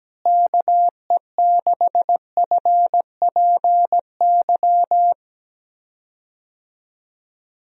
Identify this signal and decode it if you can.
Morse code, heard as a keyed tone, about 17 wpm, 700 Hz. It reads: KE6FPY
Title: USA Call Signs - 6 Characters - Repeated 3x - 17wpm